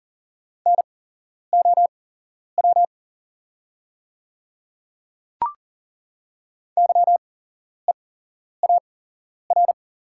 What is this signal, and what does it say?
Morse code, heard as a keyed tone, about 40 wpm, spaced as for 11 wpm, 690 Hz. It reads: NOW YEAR
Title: 8x Character Spacing - Top 200 Words - 40wpm